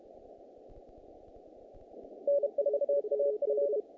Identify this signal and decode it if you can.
Morse code, heard as a keyed tone, about 31 wpm, 570 Hz. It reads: N4AF
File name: contests / naqp-cw-k5tr-op-k5pi- -2100